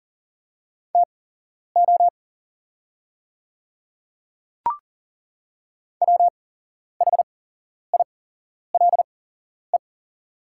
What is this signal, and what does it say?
Morse code, heard as a keyed tone, about 38 wpm, spaced as for 11 wpm, 690 Hz. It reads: TO WHILE
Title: 8x Character Spacing - Top 300 Words - 40wpm